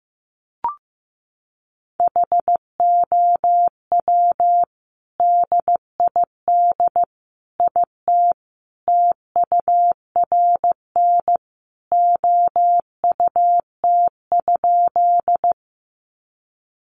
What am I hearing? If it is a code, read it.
Morse code, heard as a keyed tone, about 15 wpm, 700 Hz. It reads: HOW DID IT TURN OUT?
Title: A Sound Of Thunder 15wpm